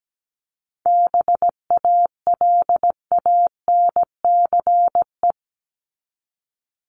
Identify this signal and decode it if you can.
Morse code, heard as a keyed tone, about 17 wpm, 700 Hz. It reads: BALANCE
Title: Top 751-1000 Words 17wpm